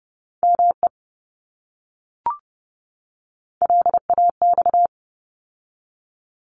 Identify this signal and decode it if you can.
Morse code, heard as a keyed tone, about 30 wpm, 700 Hz. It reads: ME LAX
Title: ARRL Sections 30wpm